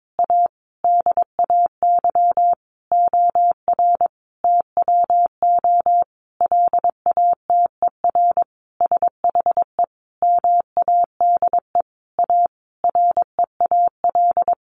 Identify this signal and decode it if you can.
Morse code, heard as a keyed tone, about 22 wpm, 700 Hz. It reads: A DAY OR TWO LATER SHE MADE A REAL
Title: Sentences from Top 1100 Words 22wpm